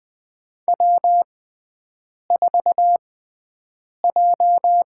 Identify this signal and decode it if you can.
Morse code, heard as a keyed tone, about 20 wpm, spaced as for 7 wpm, 700 Hz. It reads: W4J
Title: USA Call Signs - 4 Characters - 6x Character Spacing - 20wpm